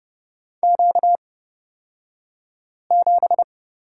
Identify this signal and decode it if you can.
Morse code, heard as a keyed tone, about 30 wpm, spaced as for 5 wpm, 700 Hz. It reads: Q7
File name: Lesson 38d - Letter q - Two Characters 30wpm